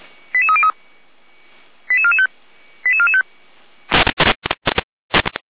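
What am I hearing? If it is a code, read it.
Five-tone selective call (ZVEI), heard as quick runs of five tones, tones of about 70 ms. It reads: 88303 88445 88445
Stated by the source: ZVEI five-tone selective call 88265 received on SDR, fire brigade Orlamünde (Feuerwehr Orlamünde).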